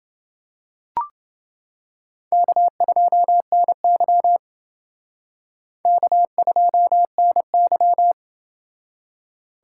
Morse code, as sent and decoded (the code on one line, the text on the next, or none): -.- ..--- -. -.-- / -.- ..--- -. -.--
K2NY K2NY